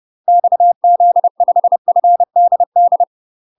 Morse code, heard as a keyed tone, about 30 wpm, 700 Hz. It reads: XZ5FDD
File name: UK MASTER QSOs TRAINING / 30WPM / UK MASTER QSOs 0034